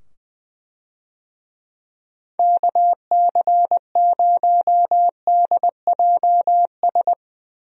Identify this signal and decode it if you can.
Morse code, heard as a keyed tone, about 20 wpm, 700 Hz. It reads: KC0DJS
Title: USA Call Signs - 6 Characters - 20wpm